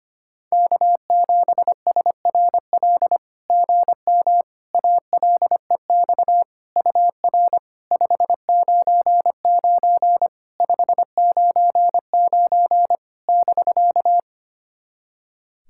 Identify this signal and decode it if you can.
Morse code, heard as a keyed tone, about 25 wpm, 700 Hz. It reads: K7SRL GM ALEX UR 599 599 <BK>
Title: SOTA QSO Practice 25wpm